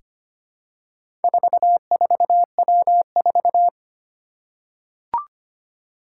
Morse code, heard as a keyed tone, about 25 wpm, 700 Hz. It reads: 44W4
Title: Four Challenging Letter-Number Combinations 25wpm